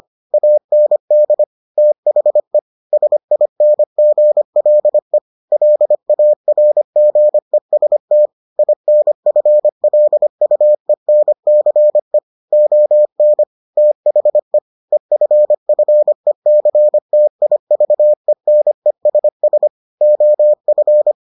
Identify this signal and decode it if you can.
Morse code, heard as a keyed tone, about 25 wpm, 600 Hz. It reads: AND THE SINGLE LARGEST INFLUENCE ON THE EFFECTIVENESS OF